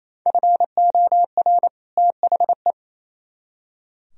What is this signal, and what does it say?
Morse code, heard as a keyed tone, about 35 wpm, 700 Hz. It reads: FOR THE
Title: Speed-Racing - Sets of 2 Words 28wpm